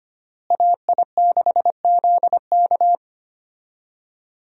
Morse code, heard as a keyed tone, about 25 wpm, 700 Hz. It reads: AI6ZK
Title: USA Call Signs - 5 Characters - Repeated 3x - 25wpm